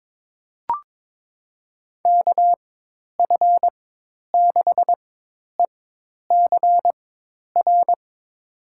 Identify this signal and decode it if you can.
Morse code, heard as a keyed tone, about 22 wpm, spaced as for 10 wpm, 700 Hz. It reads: KF6ECR
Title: USA Call Signs - 6 Characters - 4x Character Spacing - 22wpm